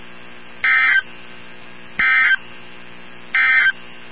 Emergency Alert System (SAME) tones, heard as harsh buzzing bursts. It is heard from 0.5 to 3.8 s.
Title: NOAA Weather Radio 162.550